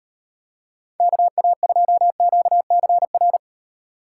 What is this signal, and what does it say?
Morse code, heard as a keyed tone, about 38 wpm, 700 Hz. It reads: KA2QCR